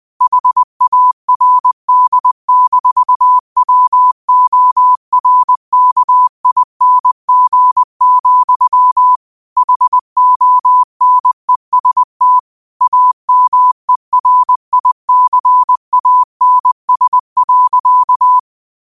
Morse code, heard as a keyed tone, about 20 wpm, 1000 Hz. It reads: HARD-WORKING, HONEST AMERICANS.